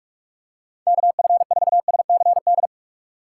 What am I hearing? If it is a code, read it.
Morse code, heard as a keyed tone, about 45 wpm, 700 Hz. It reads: KF4SCD
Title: USA Call Signs - 4x Character Spacing - 45wpm